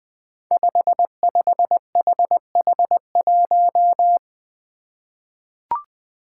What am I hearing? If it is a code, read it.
Morse code, heard as a keyed tone, about 20 wpm, 700 Hz. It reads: 55HH1
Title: Five Challenging Letter-Number Combinations 20wpm